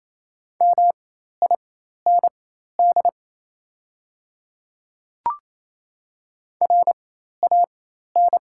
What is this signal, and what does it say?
Morse code, heard as a keyed tone, about 28 wpm, spaced as for 13 wpm, 700 Hz. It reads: MIND RAN
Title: 4x Character Spacing - Top 500 Words - 28wpm